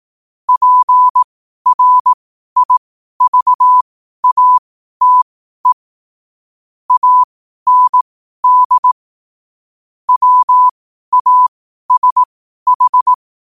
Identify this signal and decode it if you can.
Morse code, heard as a keyed tone, about 18 wpm, spaced as for 12 wpm, 1000 Hz. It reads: PRIVATE AND WASH
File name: qotdfiles / 12wpm